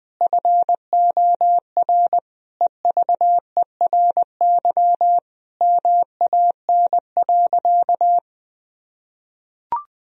Morse code, heard as a keyed tone, about 20 wpm, 700 Hz. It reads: FOR EVERY MAN.
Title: Sentences from Top 1000 Words - Review Part 1 - 20wpm